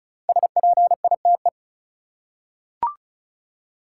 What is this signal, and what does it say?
Morse code, heard as a keyed tone, about 36 wpm, 700 Hz. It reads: SPITE
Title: Five Letter Words - 35wpm